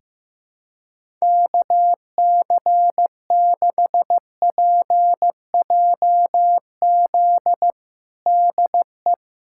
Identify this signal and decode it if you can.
Morse code, heard as a keyed tone, about 15 wpm, 700 Hz. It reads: KC6PJZ DE
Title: USA Callsign DE Callsign 15wpm